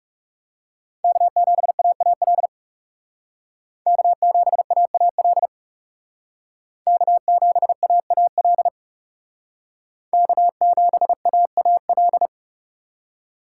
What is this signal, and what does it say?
Morse code, heard as a keyed tone, about 39 wpm, 700 Hz. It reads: K7AAL K7AAL K7AAL K7AAL